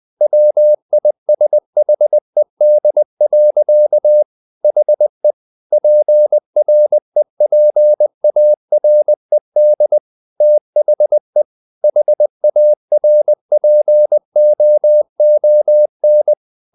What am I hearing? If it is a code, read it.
Morse code, heard as a keyed tone, about 20 wpm, 600 Hz. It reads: WISHED. HE PREPARED THE HARPOON
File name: the old man and the sea part 2 20wpm